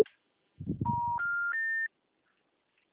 Special information tone.
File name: out- -2011- - - .124